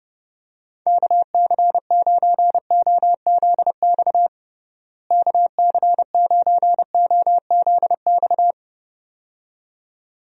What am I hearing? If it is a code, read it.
Morse code, heard as a keyed tone, about 32 wpm, 700 Hz. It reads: KC9OZX KC9OZX
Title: USA Call Signs - 6 Characters - Repeated 2x - 30wpm